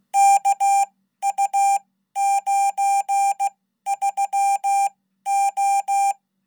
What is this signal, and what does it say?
Morse code, heard as a keyed tone, about 16 wpm, spaced as for 12 wpm, 800 Hz. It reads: KU93O
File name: Code